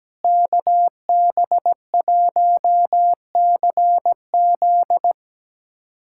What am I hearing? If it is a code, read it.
Morse code, heard as a keyed tone, about 17 wpm, 700 Hz. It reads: KB1CZ
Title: USA Callsign DE Callsign - 5 Characters - 17wpm